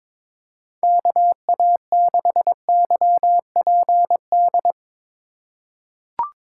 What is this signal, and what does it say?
Morse code, heard as a keyed tone, about 21 wpm, 700 Hz. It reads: KA6YPD
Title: USA Call Signs - 6 Characters - 4x Character Spacing - 22wpm